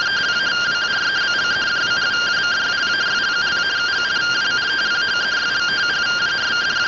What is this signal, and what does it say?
Two-tone teleprinter signal, about 170 Hz shift.